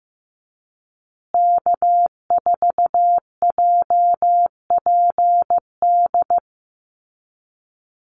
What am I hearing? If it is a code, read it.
Morse code, heard as a keyed tone, about 15 wpm, 700 Hz. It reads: K4JPD
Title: Call Signs Repeated 2x - 15wpm